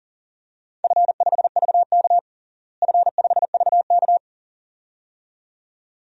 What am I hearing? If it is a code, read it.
Morse code, heard as a keyed tone, about 37 wpm, 700 Hz. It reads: F5VK F5VK
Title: DXCC-100 Call Signs - 4 Characters - Repeated 2x - 40wpm